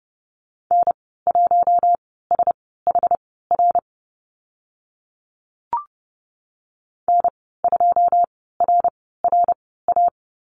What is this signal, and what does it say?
Morse code, heard as a keyed tone, about 31 wpm, spaced as for 17 wpm, 700 Hz. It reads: N1SHR N2RRA
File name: Lesson 14g - Number 2 - Call Signs 30wpm